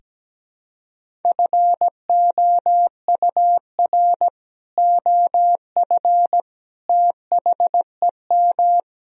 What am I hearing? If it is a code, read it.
Morse code, heard as a keyed tone, about 17 wpm, 700 Hz. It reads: FOUR OF THEM
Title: Sentences from Top 1000 Words - Review Part 1 - 17wpm